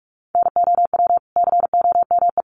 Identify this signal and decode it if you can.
Morse code, heard as a keyed tone, about 44 wpm, 700 Hz. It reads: NOW COME
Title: Sentences from Top 200 Words 45wpm